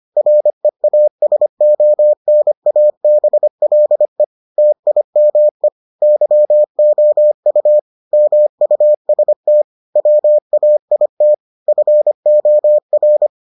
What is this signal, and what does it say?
Morse code, heard as a keyed tone, about 25 wpm, 600 Hz. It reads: REASONABLE TIME YOU MUST WAIT FOR